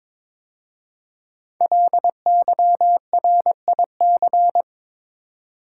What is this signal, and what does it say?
Morse code, heard as a keyed tone, about 22 wpm, 700 Hz. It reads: LYRIC
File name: 5 Letter Words - First 3 Letters Given - 22wpm